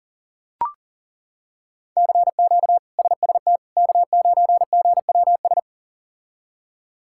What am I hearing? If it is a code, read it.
Morse code, heard as a keyed tone, about 40 wpm, 700 Hz. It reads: CQ SST K9GWS
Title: K1USN SST QSO Practice - 40wpm